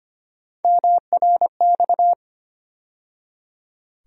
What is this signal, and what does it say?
Morse code, heard as a keyed tone, about 25 wpm, 700 Hz. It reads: MRX
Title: USA Call Sign Postfixes - 3 Letters 25wpm